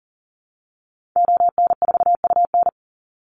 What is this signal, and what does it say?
Morse code, heard as a keyed tone, about 40 wpm, 700 Hz. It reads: ON4UN
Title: Call Signs - 40wpm